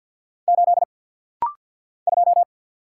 Morse code, heard as a keyed tone, about 50 wpm, 700 Hz. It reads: 8 2
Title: Single Letter-Number - Rapid-Fire 50wpm